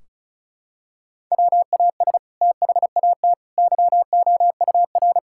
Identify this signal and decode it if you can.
Morse code, heard as a keyed tone, about 36 wpm, 700 Hz. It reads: WAS THAT YOUR
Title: Sentences from Top 1600 Words 35wpm